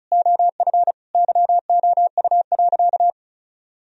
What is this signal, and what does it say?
Morse code, heard as a keyed tone, about 36 wpm, 700 Hz. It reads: OF YOU.
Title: Sentences from Top 1000 Words - Review Part 2 - 35wpm